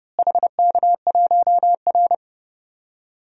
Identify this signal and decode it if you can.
Morse code, heard as a keyed tone, about 30 wpm, 700 Hz. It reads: HK1R
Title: DXCC-100 Call Signs - 4 Characters - Repeated 3x - 30wpm